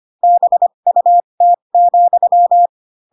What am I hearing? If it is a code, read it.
Morse code, heard as a keyed tone, about 25 wpm, spaced as for 22 wpm, 700 Hz. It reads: BUT,
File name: Pinocchio / Pinocchio ch7 18 25 ik1hge